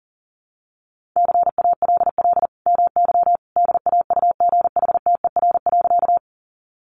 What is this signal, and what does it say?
Morse code, heard as a keyed tone, about 40 wpm, 700 Hz. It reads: CALL MY DAUGHTER.
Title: Sentences from Top 1500 Words 40wpm